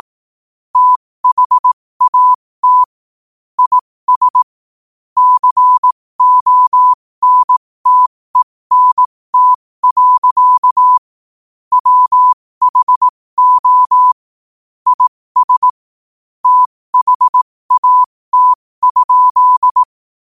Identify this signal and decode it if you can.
Morse code, heard as a keyed tone, about 18 wpm, spaced as for 15 wpm, 1000 Hz. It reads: THAT IS CONTENT. WHO IS THAT?
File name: qotdfiles / 15wpm